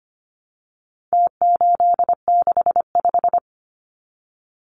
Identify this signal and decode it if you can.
Morse code, heard as a keyed tone, about 25 wpm, 700 Hz. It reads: T865
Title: Numbers - Contest Style - 4-Digit 25wpm